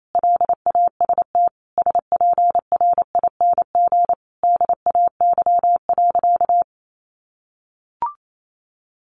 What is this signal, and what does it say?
Morse code, heard as a keyed tone, about 28 wpm, 700 Hz. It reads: LAST SPRING DAY.
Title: Sentences from Top 1200 Words 28wpm